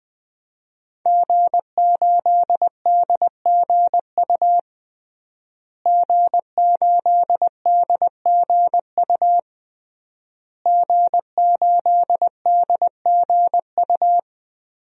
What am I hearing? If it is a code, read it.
Morse code, heard as a keyed tone, about 20 wpm, 700 Hz. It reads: G8DGU G8DGU G8DGU
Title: DXCC-100 Call Signs - 5 Characters - Repeated 3x - 20wpm